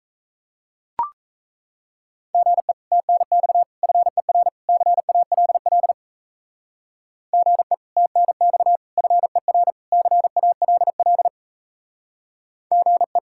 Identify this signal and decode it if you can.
Morse code, heard as a keyed tone, about 33 wpm, 700 Hz. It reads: GE TNX FER CALL GE TNX FER CALL GE
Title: Speed-Racing Common QSO Phrases - 28wpm